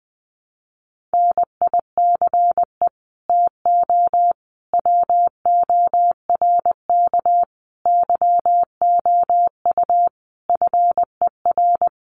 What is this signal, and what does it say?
Morse code, heard as a keyed tone, about 20 wpm, 700 Hz. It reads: NICE TO WORK YOU FER
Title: Common QSO Phrases 20wpm